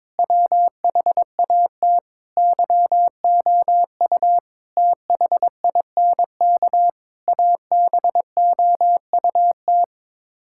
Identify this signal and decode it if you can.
Morse code, heard as a keyed tone, about 22 wpm, 700 Hz. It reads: WHAT YOU THINK ABOUT
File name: practice / 22wpm / Sets of 8 Words - 22wpm